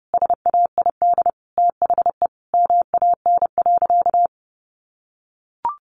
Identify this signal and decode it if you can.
Morse code, heard as a keyed tone, about 30 wpm, 700 Hz. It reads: SAID THE MAN.
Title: Sentences from Top 400 Words 30wpm